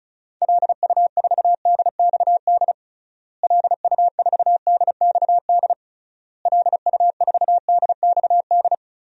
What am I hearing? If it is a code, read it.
Morse code, heard as a keyed tone, about 36 wpm, 700 Hz. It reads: LU4DXD LU4DXD LU4DXD
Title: DXCC-100 Call Signs - 6 Characters - Repeated 3x - 35wpm